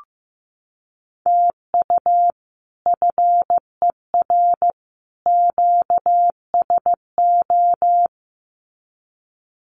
Morse code, heard as a keyed tone, about 15 wpm, 700 Hz. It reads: TU FER QSO
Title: Common QSO Phrases 15wpm